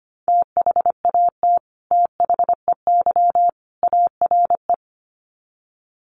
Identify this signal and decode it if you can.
Morse code, heard as a keyed tone, about 25 wpm, 700 Hz. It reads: THAT THEY ARE
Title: Sets of 3 Words 25wpm